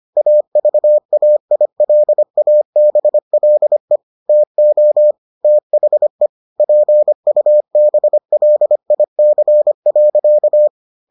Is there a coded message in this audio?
Yes: Morse code from 0.2 to 10.7 s, heard as a keyed tone, about 25 wpm, 600 Hz. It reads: AVAILABLE TO THE PUBLIC.